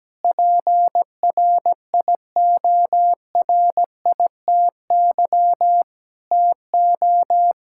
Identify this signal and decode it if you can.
Morse code, heard as a keyed tone, about 17 wpm, 700 Hz. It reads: PRIORITY TO
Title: Sentences from Top 1600 Words 17wpm